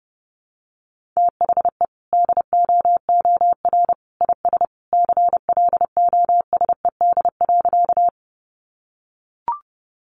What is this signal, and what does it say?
Morse code, heard as a keyed tone, about 30 wpm, 700 Hz. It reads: THE DOOR IS CLOSED.
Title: Sentences from Top 1400 Words 30wpm